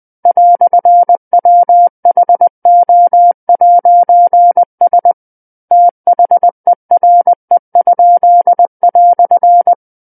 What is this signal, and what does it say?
Morse code, heard as a keyed tone, about 20 wpm, 700 Hz. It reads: "WHO'S THERE?"
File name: 20 WPM